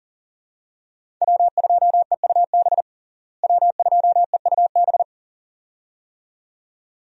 Morse code, heard as a keyed tone, about 40 wpm, 700 Hz. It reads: W2EUB W2EUB
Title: USA Call Signs - 5 Characters - Repeated 2x - 40wpm